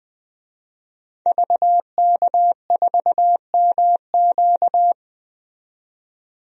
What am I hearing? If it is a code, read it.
Morse code, heard as a keyed tone, about 20 wpm, 700 Hz. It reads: VK4MQ
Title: DXCC-100 Call Signs - 5 Characters - 20wpm